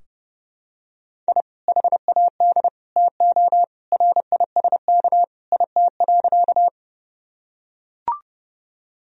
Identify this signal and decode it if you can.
Morse code, heard as a keyed tone, about 32 wpm, 700 Hz. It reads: I HAD TO RISK IT.
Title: Sentences from Top 1000 Words - Review Part 4 - 30wpm